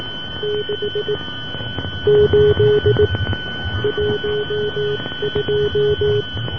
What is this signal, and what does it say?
Morse code, heard as a keyed tone, about 18 wpm, spaced as for 16 wpm, 420 Hz, with noise. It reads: 6 8 12